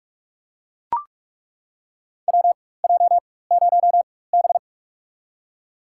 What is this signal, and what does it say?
Morse code, heard as a keyed tone, about 43 wpm, spaced as for 21 wpm, 700 Hz. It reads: WJ0B
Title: USA Call Signs - 4x Character Spacing - 45wpm